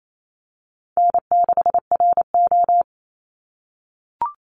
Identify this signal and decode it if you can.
Morse code, heard as a keyed tone, about 28 wpm, 700 Hz. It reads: N6RO
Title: Easy Call Signs - 28wpm